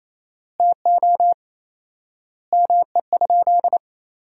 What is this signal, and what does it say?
Morse code, heard as a keyed tone, about 28 wpm, 700 Hz. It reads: TO ME?
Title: 4x Word-Spacing - Sentences from Top 200 Words - 28wpm